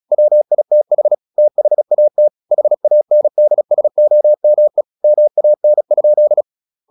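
Morse code, heard as a keyed tone, about 37 wpm, 600 Hz. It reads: WITH THAT HANDSOME MAN?
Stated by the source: Morse practice at 36 wpm, text of Ken Follett's novel Never, from CW med Gnister.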